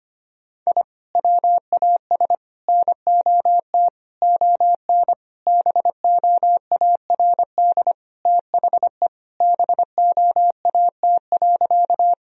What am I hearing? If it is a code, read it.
Morse code, heard as a keyed tone, about 25 wpm, 700 Hz. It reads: I WAS NOT ON BOARD THE BOAT.
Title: Sentences from Top 1000 Words - Review Part 4 - 25wpm